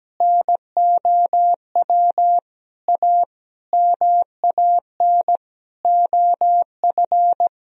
Morse code, heard as a keyed tone, about 17 wpm, 700 Hz. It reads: NOW A MAN OF
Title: Sentences from Top 1000 Words - Review Part 3 - 17wpm